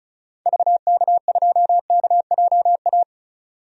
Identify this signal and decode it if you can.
Morse code, heard as a keyed tone, about 34 wpm, 700 Hz. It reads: VK2KJA
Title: DXCC-100 Call Signs - 6 Characters - 35wpm